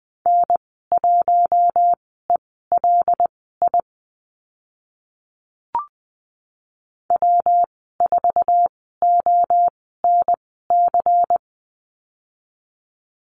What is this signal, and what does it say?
Morse code, heard as a keyed tone, about 20 wpm, spaced as for 15 wpm, 700 Hz. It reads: N1ELI W4ONC
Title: 140 Lesson 23g - Letter f - Call Signs 20wpm